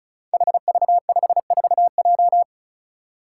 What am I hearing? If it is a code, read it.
Morse code, heard as a keyed tone, about 36 wpm, 700 Hz. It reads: HV54J
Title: Five Challenging Letter-Number Combinations 35wpm